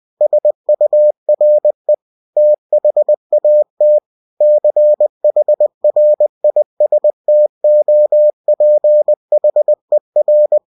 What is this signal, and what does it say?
Morse code, heard as a keyed tone, about 20 wpm, 600 Hz. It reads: SURE THAT CHRISTOPHER
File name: cw-books / winnie-the-pooh / winnie-20wpm-page-0071